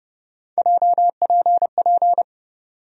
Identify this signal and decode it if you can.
Morse code, heard as a keyed tone, about 30 wpm, 700 Hz. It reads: JPP
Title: USA Call Sign Postfixes - 3 Letters 30wpm